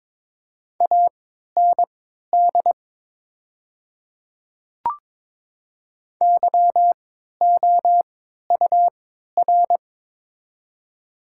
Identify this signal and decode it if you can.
Morse code, heard as a keyed tone, about 22 wpm, spaced as for 12 wpm, 700 Hz. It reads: AND YOUR
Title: 3x Character Spacing - Top 100 Words - 22wpm